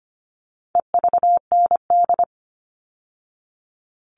Morse code, heard as a keyed tone, about 25 wpm, 700 Hz. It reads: EVND
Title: All Cut Numbers - 4-Digit 25wpm